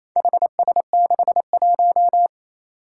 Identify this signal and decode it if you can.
Morse code, heard as a keyed tone, about 28 wpm, 700 Hz. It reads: HS61
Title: Four Challenging Letter-Number Combinations 28wpm